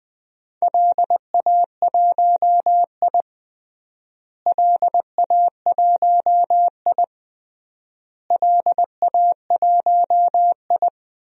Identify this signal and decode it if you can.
Morse code, heard as a keyed tone, about 20 wpm, 700 Hz. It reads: LA1I LA1I LA1I
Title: DXCC-100 Call Signs - 4 Characters - Repeated 3x - 20wpm